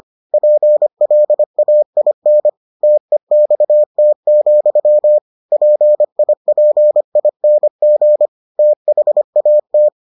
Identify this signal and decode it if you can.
Morse code, heard as a keyed tone, about 25 wpm, 600 Hz. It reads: PLAIN TEXT, PIPING THAT